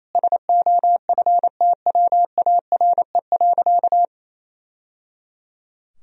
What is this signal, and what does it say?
Morse code, heard as a keyed tone, about 28 wpm, 700 Hz. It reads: SOFTWARE.
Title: Sentences from Top 1000 Words - Review Part 3 - 28wpm